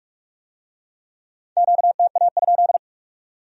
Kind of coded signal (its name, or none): Morse code